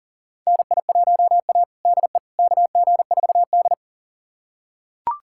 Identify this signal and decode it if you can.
Morse code, heard as a keyed tone, about 40 wpm, 700 Hz. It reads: NI1A DE KG4D
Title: USA Callsign DE Callsign - 4 Characters - 40wpm